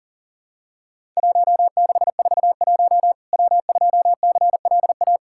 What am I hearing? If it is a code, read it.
Morse code, heard as a keyed tone, about 40 wpm, 700 Hz. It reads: 1641 W2CLA
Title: Worked All Europe DX Contest - QTC Exchange - 40wpm